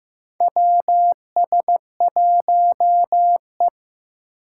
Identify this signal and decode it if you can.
Morse code, heard as a keyed tone, about 15 wpm, 700 Hz. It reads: WS1E
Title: USA Call Signs - 4 Characters - 6x Character Spacing - 15wpm